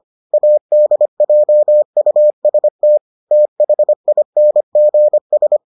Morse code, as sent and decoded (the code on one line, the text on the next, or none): .- -.. .--- ..- ... - / - .... .. -. --. ...
ADJUST THINGS